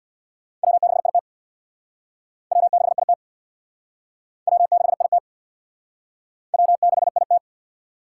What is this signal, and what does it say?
Morse code, heard as a keyed tone, about 65 wpm, 700 Hz. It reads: W6IT W6IT W6IT W6IT